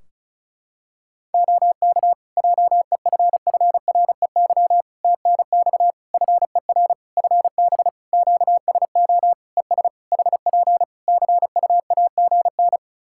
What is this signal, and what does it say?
Morse code, heard as a keyed tone, about 34 wpm, 700 Hz. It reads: OK JEFFREY TNX FER FB QSO ES HP CUAGN
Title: Practice QSO - Exchange 4 - Ending - 35wpm